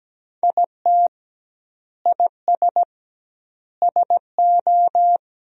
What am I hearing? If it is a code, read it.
Morse code, heard as a keyed tone, about 17 wpm, 700 Hz. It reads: IT IS SO
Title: 2x Word-Spacing - Sentences from Top 300 Words - 17wpm